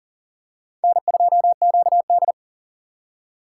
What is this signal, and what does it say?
Morse code, heard as a keyed tone, about 40 wpm, 700 Hz. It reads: N2QD
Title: USA Call Signs - 4 Characters - 6x Character Spacing - 40wpm